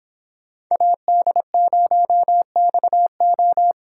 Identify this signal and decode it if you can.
Morse code, heard as a keyed tone, about 26 wpm, 700 Hz. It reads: AD0XO
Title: Speed Racing - USA 2x2 Call Signs 22wpm